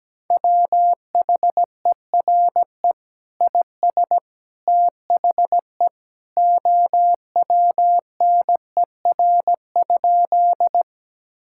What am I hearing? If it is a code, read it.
Morse code, heard as a keyed tone, about 17 wpm, 700 Hz. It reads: WHERE IS THE OWNER?
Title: Sentences from Top 1600 Words 17wpm